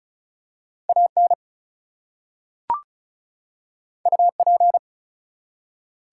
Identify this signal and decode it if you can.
Morse code, heard as a keyed tone, about 36 wpm, 700 Hz. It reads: AN UP
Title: Two Letter Words - 35wpm